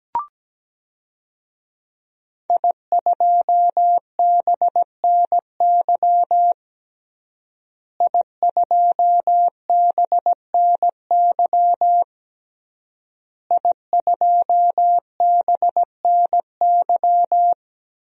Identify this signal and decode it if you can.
Morse code, heard as a keyed tone, about 17 wpm, 700 Hz. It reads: I2BNY I2BNY I2BNY
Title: DXCC-100 Call Signs - 5 Characters - Repeated 3x - 17wpm